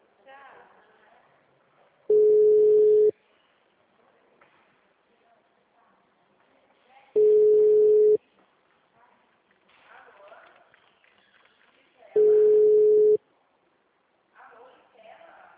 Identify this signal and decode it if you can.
Ringback tone (Europe, 425 Hz), with speech.
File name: out- -2013- - - .20062